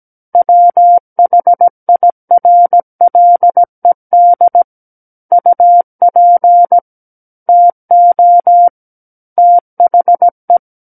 Morse code, heard as a keyed tone, about 17 wpm, 700 Hz. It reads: WHIRLED UP TO THE
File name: Book One / 08-Chapter 8 - 17 WPM